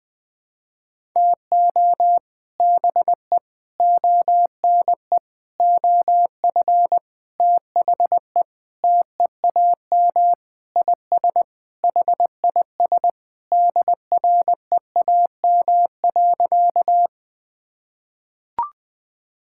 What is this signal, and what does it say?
Morse code, heard as a keyed tone, about 20 wpm, 700 Hz. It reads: TO BE ONE OF THE TEAM IS HIS DREAM.
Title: Sentences with 4-Word Phrases - 20wpm